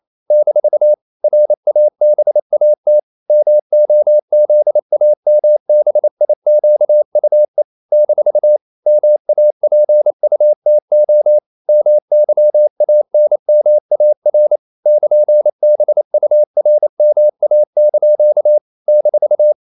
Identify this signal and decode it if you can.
Morse code, heard as a keyed tone, about 28 wpm, 600 Hz. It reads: - RABAT MOZAMBIQUE - MAPUTO MYANMAR <KN>BURMA) -
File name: CW / Practice / NationalCapitals28